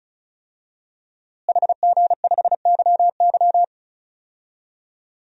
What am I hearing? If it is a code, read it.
Morse code, heard as a keyed tone, about 34 wpm, 700 Hz. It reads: HG5YY
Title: DXCC-100 Call Signs - 5 Characters - Repeated 3x - 35wpm